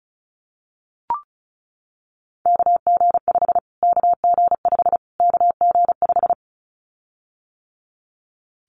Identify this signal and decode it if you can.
Morse code, heard as a keyed tone, about 34 wpm, 700 Hz. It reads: KG5 KG5 KG5
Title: Callsign Prefixes for Top 25 DXCC - Repeated 3x - 35wpm